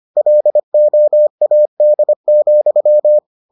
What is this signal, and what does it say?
Morse code, heard as a keyed tone, about 25 wpm, 600 Hz. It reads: LOAD,